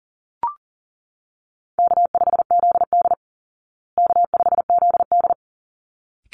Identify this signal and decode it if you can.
Morse code, heard as a keyed tone, about 40 wpm, 700 Hz. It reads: K5ZD K5ZD
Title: Easy Call Signs Repeated 2x - 40wpm